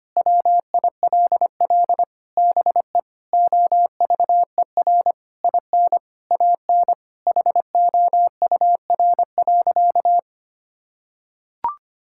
Morse code, heard as a keyed tone, about 25 wpm, 700 Hz. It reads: WILL BE OVER IN AN HOUR.
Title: Sentences from Top 1000 Words - Review Part 5 - 25wpm